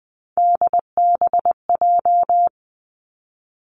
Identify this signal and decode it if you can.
Morse code, heard as a keyed tone, about 20 wpm, 700 Hz. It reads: DBJ
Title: Three Challenging Letter-Number Combinations 20wpm